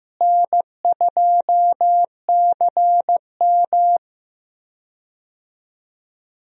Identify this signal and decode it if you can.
Morse code, heard as a keyed tone, about 15 wpm, 700 Hz. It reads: N2CM
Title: USA Call Signs - 4 Characters - Repeated 3x - 15wpm